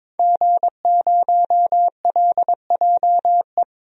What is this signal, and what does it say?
Morse code, heard as a keyed tone, about 22 wpm, 700 Hz. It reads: G0LJE
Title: DXCC-100 Call Signs - 5 Characters - Repeated 2x - 22wpm